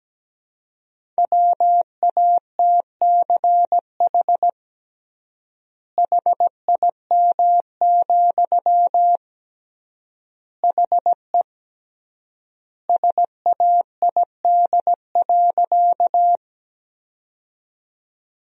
Morse code, heard as a keyed tone, about 17 wpm, 700 Hz. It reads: WATCH HIM, HE SAID.
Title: 3x Word-Spacing - Sentences from Top 300 Words - 17wpm